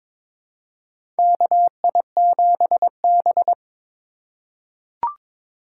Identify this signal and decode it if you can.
Morse code, heard as a keyed tone, about 23 wpm, 700 Hz. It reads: KI7B
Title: USA Call Signs - 4 Characters - 2x Character Spacing - 22wpm